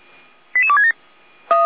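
Five-tone selective call (ZVEI), heard as quick runs of five tones, tones of about 70 ms. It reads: 88267
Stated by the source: Received by SDR as a ZVEI five-tone call.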